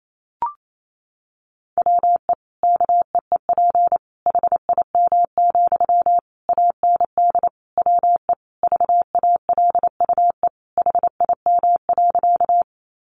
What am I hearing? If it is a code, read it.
Morse code, heard as a keyed tone, about 28 wpm, 700 Hz. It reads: WE KEEP HIM, AND WE VALUE HIM.
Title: Sentences from Top 600 Words 28wpm